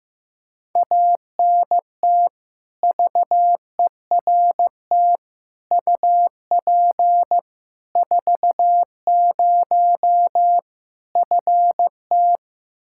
Morse code, heard as a keyed tone, about 15 wpm, 700 Hz. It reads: ANT VERT UP 40 FT
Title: Practice QSO - Exchange 2 - Info - 15wpm